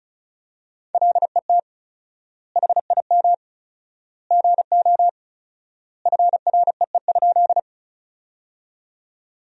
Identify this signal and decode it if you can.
Morse code, heard as a keyed tone, about 36 wpm, 700 Hz. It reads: LET HIM GO FREE?
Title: 4x Word-Spacing - Sentences from Top 500 Words - 35wpm